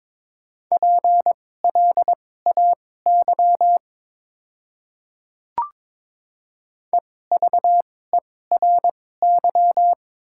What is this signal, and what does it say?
Morse code, heard as a keyed tone, about 22 wpm, spaced as for 16 wpm, 700 Hz. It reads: PLAY EVERY
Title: 2x Character Spacing - Top 200 Words - 22wpm